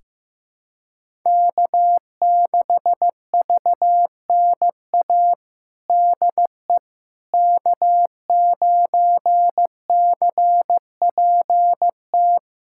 Morse code, heard as a keyed tone, about 15 wpm, 700 Hz. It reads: K6VNA DE K9CPT
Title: USA Callsign DE Callsign - 5 Characters - 15wpm